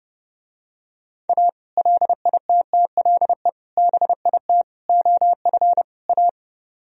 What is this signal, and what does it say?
Morse code, heard as a keyed tone, about 31 wpm, 700 Hz. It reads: A LITTLE BIT OF A